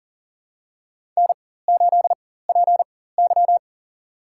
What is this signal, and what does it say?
Morse code, heard as a keyed tone, about 40 wpm, spaced as for 19 wpm, 700 Hz. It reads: N8PY
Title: USA Call Signs - 4 Characters - 4x Character Spacing - 40wpm